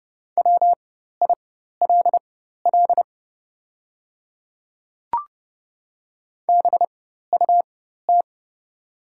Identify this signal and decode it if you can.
Morse code, heard as a keyed tone, about 30 wpm, spaced as for 14 wpm, 690 Hz. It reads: WILL BUT
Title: 4x Character Spacing - Top 100 Words - 30wpm